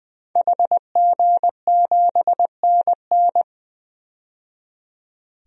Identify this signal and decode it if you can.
Morse code, heard as a keyed tone, about 20 wpm, 700 Hz. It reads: HG7NN
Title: DXCC-100 Call Signs - 5 Characters - 20wpm